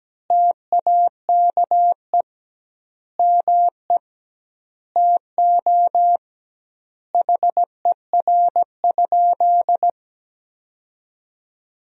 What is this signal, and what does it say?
Morse code, heard as a keyed tone, about 17 wpm, 700 Hz. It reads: TAKE ME TO HER?
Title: 2x Word-Spacing - Sentences from Top 200 Words - 17wpm